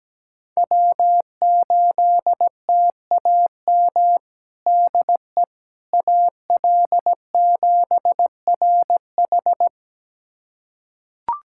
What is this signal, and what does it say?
Morse code, heard as a keyed tone, about 17 wpm, 700 Hz. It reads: W8TAM DE AL7RH
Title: USA Callsign DE Callsign - 5 Characters - 17wpm